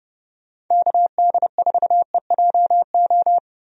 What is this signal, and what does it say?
Morse code, heard as a keyed tone, about 30 wpm, 700 Hz. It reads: KD4EJO
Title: USA Call Signs - 6 Characters - Repeated 3x - 30wpm